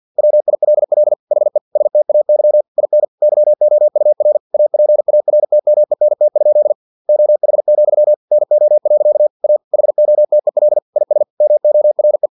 Morse code, heard as a keyed tone, about 50 wpm, 600 Hz. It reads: WILL HE STAY IN YOUR APARTMENT? OH, NO. A HOTEL IS MORE
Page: Never 49wpm